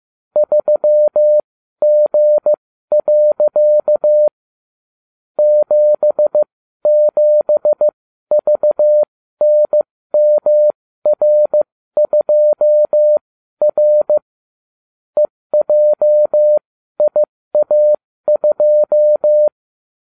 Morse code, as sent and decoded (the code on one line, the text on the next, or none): ...-- --. .-.-.- / --... --... ...- -. -- .-. ..--- .-. / . .--- .. .- ..---
3G. 77VNMR2R EJIA2